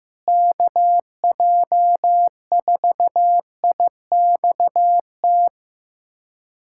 Morse code, heard as a keyed tone, about 15 wpm, 700 Hz. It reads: KJ4IXT
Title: USA Callsign DE Callsign - 6 Characters - 15wpm